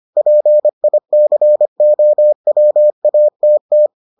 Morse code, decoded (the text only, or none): PICOWATT